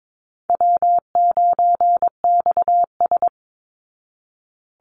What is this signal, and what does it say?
Morse code, heard as a keyed tone, about 22 wpm, 700 Hz. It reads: W9XS
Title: Call Signs Repeated 2x - 22wpm